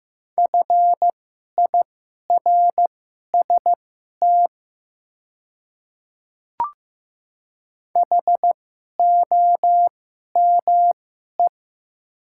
Morse code, heard as a keyed tone, about 15 wpm, spaced as for 11 wpm, 700 Hz. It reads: FIRST HOME
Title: 2x Character Spacing - Top 200 Words - 15wpm